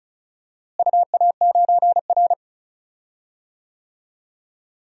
Morse code, heard as a keyed tone, about 36 wpm, 700 Hz. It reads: UA9R
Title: DXCC-100 Call Signs - 4 Characters - Repeated 3x - 35wpm